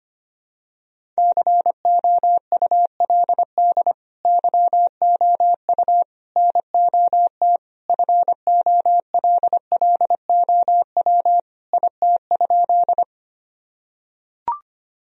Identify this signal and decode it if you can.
Morse code, heard as a keyed tone, about 25 wpm, 700 Hz. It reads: COULD YOU NOT FOLLOW IT?
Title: Sentences from Top 1000 Words - Review Part 1 - 25wpm